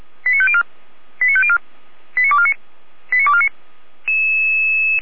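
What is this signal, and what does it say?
Five-tone selective call (ZVEI), heard as quick runs of five tones, tones of about 70 ms. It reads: 89504 89504 89249 89249
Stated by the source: SDR reception of a ZVEI five-tone selective call.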